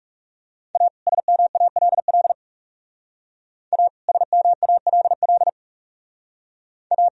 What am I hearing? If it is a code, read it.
Morse code, heard as a keyed tone, about 37 wpm, 700 Hz. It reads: A SMALL A SMALL A
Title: Speed-Racing - Sets of 2 Words 30wpm